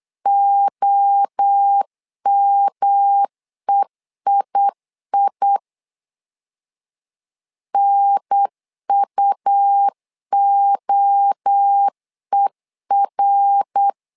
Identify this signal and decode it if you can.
Morse code, heard as a keyed tone, about 8 wpm, 800 Hz. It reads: OMEII NUOER
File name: 4 Q 8wpm